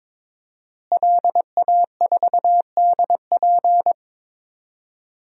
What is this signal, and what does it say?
Morse code, heard as a keyed tone, about 22 wpm, 700 Hz. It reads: LA4DP
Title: DXCC-100 Call Signs - 5 Characters - Repeated 2x - 22wpm